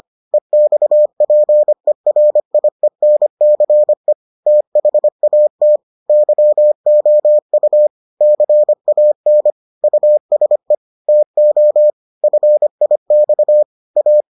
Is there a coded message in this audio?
Yes: Morse code from 0.3 to 14.2 s, heard as a keyed tone, about 25 wpm, 600 Hz. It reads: EXPERIENCE THAT YOU CAN USE TO FIX A